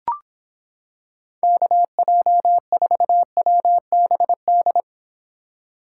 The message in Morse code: KJ4WBD

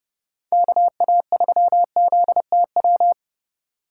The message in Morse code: KA3ZTW

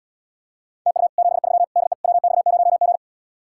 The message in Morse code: TU 73 DE WX0G

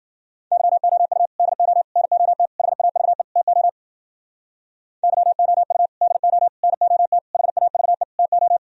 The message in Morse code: YOU DO NOT HAVE TO YOU DO NOT HAVE TO

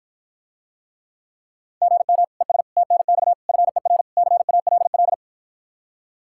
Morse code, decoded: GM ES TNX FER CALL

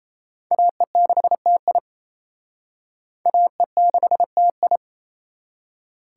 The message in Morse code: AE6TI AE6TI

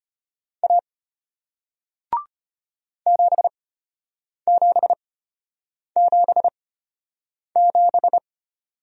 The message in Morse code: A 7777

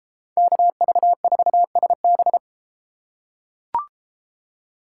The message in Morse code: KV4SB

